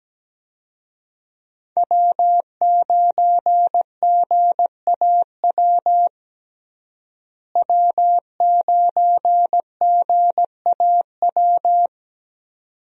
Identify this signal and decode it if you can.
Morse code, heard as a keyed tone, about 17 wpm, 700 Hz. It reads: W9GAW W9GAW
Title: USA Call Signs - 5 Characters - Repeated 2x - 17wpm